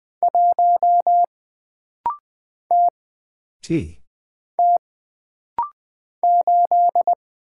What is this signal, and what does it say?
Morse code, heard as a keyed tone, about 20 wpm, 700 Hz. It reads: 1 T T 8
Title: Single Character (Letter:Number:Punctuation:Prosign) - Rapid-Fire - 20wpm